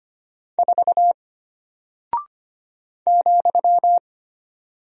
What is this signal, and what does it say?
Morse code, heard as a keyed tone, about 25 wpm, 700 Hz. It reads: 4 ,
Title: Single Character (Letter:Number:Punctuation:Prosign) Alt 25wpm